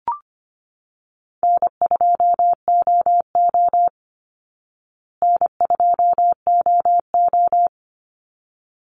Morse code, heard as a keyed tone, about 24 wpm, 700 Hz. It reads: N2OO N2OO
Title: Call Signs Repeated 3x - 25wpm